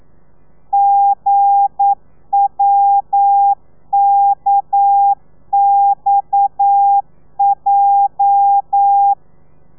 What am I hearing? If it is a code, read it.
Morse code, heard as a keyed tone, about 9 wpm, 800 Hz. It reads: GWKXJ
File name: telegrafia / i2tfj / Lez8 40